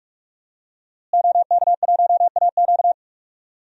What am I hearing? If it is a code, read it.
Morse code, heard as a keyed tone, about 45 wpm, 700 Hz. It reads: OK1AQ
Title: DXCC-100 Call Signs - 5 Characters - Repeated 2x - 45wpm